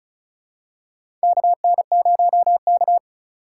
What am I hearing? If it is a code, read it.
Morse code, heard as a keyed tone, about 34 wpm, 700 Hz. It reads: KN0K
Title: USA Call Signs - 4x Character Spacing - 35wpm